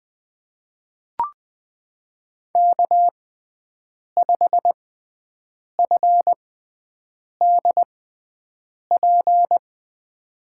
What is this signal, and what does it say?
Morse code, heard as a keyed tone, about 20 wpm, spaced as for 7 wpm, 700 Hz. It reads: K5FDP
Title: USA Call Signs - 5 Characters - 6x Character Spacing - 20wpm